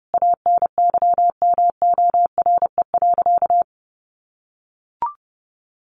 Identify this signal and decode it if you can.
Morse code, heard as a keyed tone, about 30 wpm, 700 Hz. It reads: ANYMORE.
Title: Additional Easy Phrases 30wpm